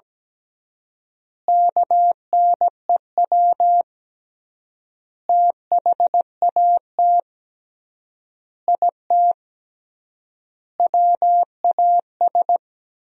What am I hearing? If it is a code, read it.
Morse code, heard as a keyed tone, about 17 wpm, 700 Hz. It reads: KNEW THAT IT WAS
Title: 3x Word-Spacing - Sentences from Top 400 Words - 17wpm